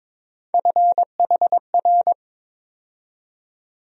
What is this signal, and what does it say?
Morse code, heard as a keyed tone, about 22 wpm, 700 Hz. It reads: FHR
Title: USA Call Sign Postfixes - 3 Letters 22wpm